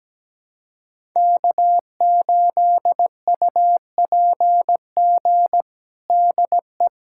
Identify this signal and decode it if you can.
Morse code, heard as a keyed tone, about 17 wpm, 700 Hz. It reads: K8UPG DE
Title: USA Callsign DE Callsign 17wpm